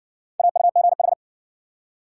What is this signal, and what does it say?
Morse code, heard as a keyed tone, about 60 wpm, 700 Hz. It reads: AUGH